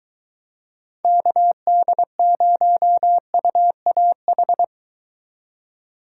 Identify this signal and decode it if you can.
Morse code, heard as a keyed tone, about 23 wpm, 700 Hz. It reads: KD0UAH